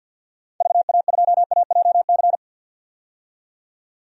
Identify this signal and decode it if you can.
Morse code, heard as a keyed tone, about 49 wpm, 700 Hz. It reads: VA2AJC
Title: DXCC-100 Call Signs - 6 Characters - 50wpm